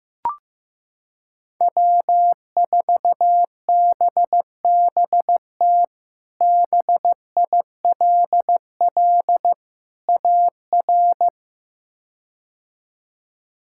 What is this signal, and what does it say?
Morse code, heard as a keyed tone, about 16 wpm, 700 Hz. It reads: W4BBT BILL AR
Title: K1USN SST QSO Practice - 15wpm